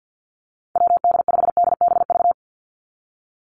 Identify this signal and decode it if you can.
Morse code, heard as a keyed tone, about 50 wpm, 700 Hz. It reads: WD5DBV
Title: Call Signs - 50wpm